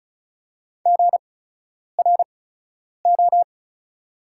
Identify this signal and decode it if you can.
Morse code, heard as a keyed tone, about 34 wpm, spaced as for 10 wpm, 700 Hz. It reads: GRO
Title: 8x Character Spacing - Top 300 Words - 35wpm